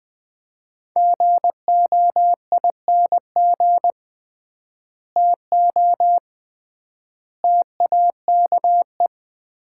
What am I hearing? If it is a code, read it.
Morse code, heard as a keyed tone, about 21 wpm, 700 Hz. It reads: GOING TO TAKE
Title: 3x Word-Spacing - Sets of 4 Words - 20wpm